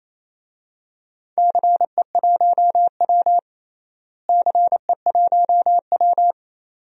Morse code, heard as a keyed tone, about 28 wpm, 700 Hz. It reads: CE1W CE1W
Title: DXCC-100 Call Signs - 4 Characters - Repeated 2x - 28wpm